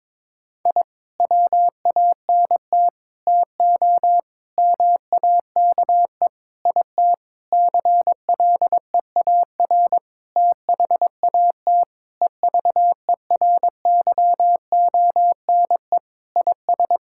Morse code, heard as a keyed tone, about 22 wpm, 700 Hz. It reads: I WANT TO MAKE IT CLEAR THAT EVERYONE IS
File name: Sentences with 7-Word Phrases - 22wpm